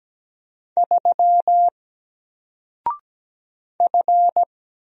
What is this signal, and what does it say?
Morse code, heard as a keyed tone, about 17 wpm, 700 Hz. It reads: 3 F
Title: Single Letter-Number Alt 17wpm